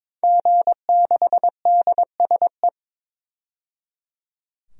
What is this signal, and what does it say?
Morse code, heard as a keyed tone, about 22 wpm, 700 Hz. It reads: G6DSE
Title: DXCC-100 Call Signs - 5 Characters - 22wpm